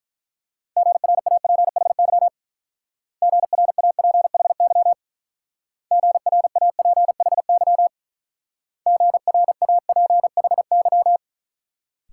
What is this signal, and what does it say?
Morse code, heard as a keyed tone, about 44 wpm, 700 Hz. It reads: GRAPHY GRAPHY GRAPHY GRAPHY